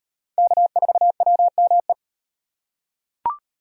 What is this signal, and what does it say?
Morse code, heard as a keyed tone, about 38 wpm, 700 Hz. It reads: K4WME